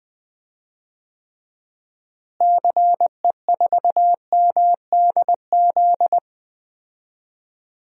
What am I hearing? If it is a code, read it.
Morse code, heard as a keyed tone, about 20 wpm, 700 Hz. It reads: CE4MDZ
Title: DXCC-100 Call Signs - 6 Characters - Repeated 3x - 20wpm